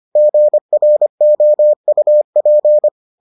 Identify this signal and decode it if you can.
Morse code, heard as a keyed tone, about 25 wpm, 600 Hz. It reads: GROUP